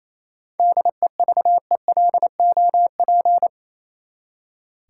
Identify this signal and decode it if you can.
Morse code, heard as a keyed tone, about 28 wpm, 700 Hz. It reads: DEVELOP
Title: Seven Letter Words - 28wpm